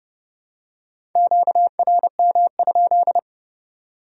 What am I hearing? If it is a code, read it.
Morse code, heard as a keyed tone, about 26 wpm, 700 Hz. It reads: QRM?